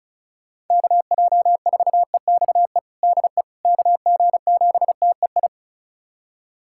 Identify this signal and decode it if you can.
Morse code, heard as a keyed tone, about 34 wpm, 690 Hz. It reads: KJ4EXE DE KG7TEI
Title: USA Callsign DE Callsign - 6 Characters - 35wpm